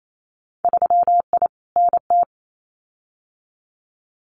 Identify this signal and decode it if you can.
Morse code, heard as a keyed tone, about 28 wpm, 700 Hz. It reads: 3I NT
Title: Winter Field Day Exchanges Encore 28wpm